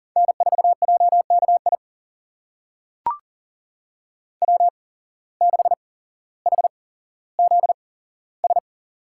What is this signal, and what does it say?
Morse code, heard as a keyed tone, about 40 wpm, spaced as for 17 wpm, 700 Hz. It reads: N4JKI W6HZS
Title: USA Call Signs - 5 Characters - 8x Character Spacing - 40wpm